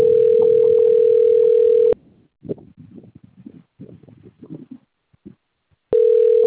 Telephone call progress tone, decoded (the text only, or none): ringback tone